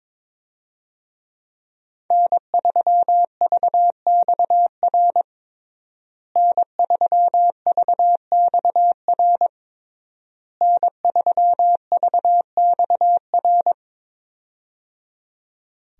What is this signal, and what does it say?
Morse code, heard as a keyed tone, about 22 wpm, 700 Hz. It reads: N3VXR N3VXR N3VXR
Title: USA Call Signs - 5 Characters - Repeated 3x - 22wpm